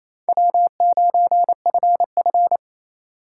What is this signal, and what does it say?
Morse code, heard as a keyed tone, about 37 wpm, 700 Hz. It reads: W9FF